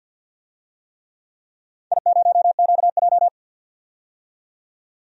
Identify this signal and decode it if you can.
Morse code, heard as a keyed tone, about 52 wpm, 700 Hz. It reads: I0QJ
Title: DXCC-100 Call Signs - 4 Characters - 50wpm